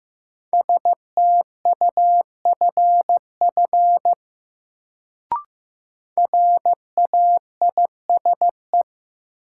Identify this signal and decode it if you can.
Morse code, heard as a keyed tone, about 15 wpm, 700 Hz. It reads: STUFF RAISE
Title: Sentences from Top 1400 Words - New Word Drill - 15wpm